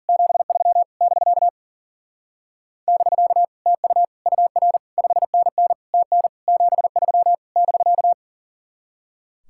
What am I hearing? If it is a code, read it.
Morse code, heard as a keyed tone, about 52 wpm, 700 Hz. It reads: 73 <BK> <BK> TU UR 5NN TN 73 <BK>